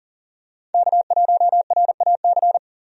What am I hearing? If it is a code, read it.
Morse code, heard as a keyed tone, about 40 wpm, 700 Hz. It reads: K1RAC
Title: USA Call Signs - 5 Characters - 40wpm